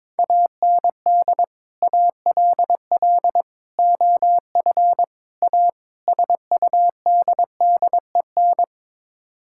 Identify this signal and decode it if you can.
Morse code, heard as a keyed tone, about 22 wpm, 700 Hz. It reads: AND ALL OF A SUDDEN
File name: practice / 22wpm / Build A Phrase - 5-Word Phrases - 22wpm